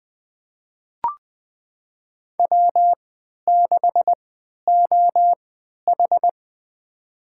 Morse code, heard as a keyed tone, about 20 wpm, spaced as for 11 wpm, 700 Hz. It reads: W6OH
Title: USA Call Signs - 4 Characters - 3x Character Spacing - 20wpm